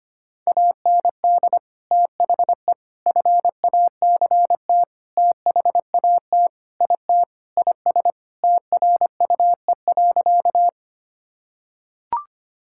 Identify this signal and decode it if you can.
Morse code, heard as a keyed tone, about 25 wpm, 700 Hz. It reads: AND THE FACT THAT IT IS TRUE.
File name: practice / 25wpm / Sentences with 4-Word Phrases - 25wpm